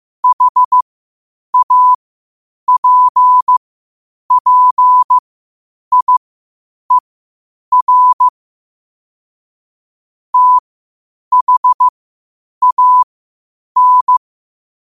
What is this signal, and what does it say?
Morse code, heard as a keyed tone, about 15 wpm, spaced as for 8 wpm, 1000 Hz. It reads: HAPPIER THAN